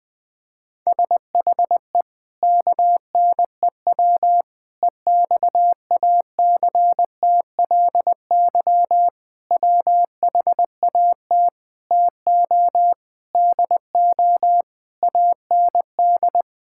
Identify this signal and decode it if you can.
Morse code, heard as a keyed tone, about 20 wpm, 700 Hz. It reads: SHE KNEW EXACTLY WHAT TO DO AND
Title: Sentences with 8-Word Phrases - 20wpm